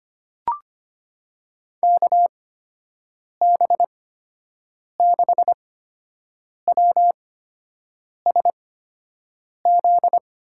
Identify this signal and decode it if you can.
Morse code, heard as a keyed tone, about 25 wpm, spaced as for 7 wpm, 700 Hz. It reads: KB6WSZ
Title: USA Call Signs - 6 Characters - 8x Character Spacing - 25wpm